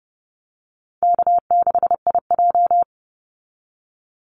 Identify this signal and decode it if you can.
Morse code, heard as a keyed tone, about 30 wpm, 700 Hz. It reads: K6IJ
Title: Easy Call Signs Repeated 2x - 30wpm